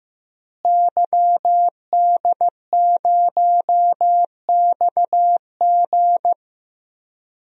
Morse code, heard as a keyed tone, about 15 wpm, 700 Hz. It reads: YD0XG
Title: DXCC-100 Call Signs - 5 Characters - Repeated 2x - 15wpm